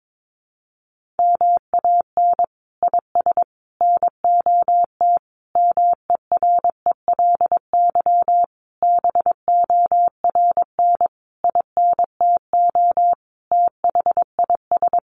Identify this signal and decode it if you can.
Morse code, heard as a keyed tone, about 22 wpm, 700 Hz. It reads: MAN IS NOT MERELY BORN INTO THIS
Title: Sentences from Top 1400 Words 22wpm